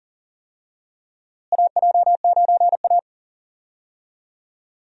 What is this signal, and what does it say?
Morse code, heard as a keyed tone, about 40 wpm, 700 Hz. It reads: AJ9A
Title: USA Call Signs - 4 Characters - Repeated 3x - 40wpm